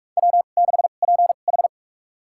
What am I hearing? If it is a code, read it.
Morse code, heard as a keyed tone, about 46 wpm, spaced as for 32 wpm, 700 Hz. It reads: W6PH
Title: USA Call Signs - 2x Character Spacing - 45wpm